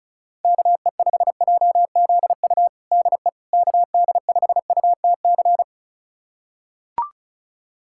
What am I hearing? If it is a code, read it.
Morse code, heard as a keyed tone, about 36 wpm, 700 Hz. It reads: KE5JZU DE KD5UTC
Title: USA Callsign DE Callsign 35wpm